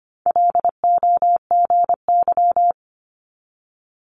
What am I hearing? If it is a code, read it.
Morse code, heard as a keyed tone, about 25 wpm, 700 Hz. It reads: LOGY
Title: Multi-letter Phonemes 25wpm